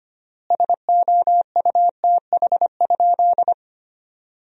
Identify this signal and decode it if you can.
Morse code, heard as a keyed tone, about 25 wpm, 700 Hz. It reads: SOUTH?
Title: 3x Word-Spacing - Sentences from Top 400 Words - 25wpm